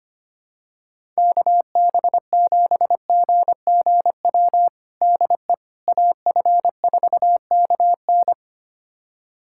Morse code, heard as a keyed tone, about 25 wpm, 700 Hz. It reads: KB7GGW DE AF4KN
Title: USA Callsign DE Callsign 25wpm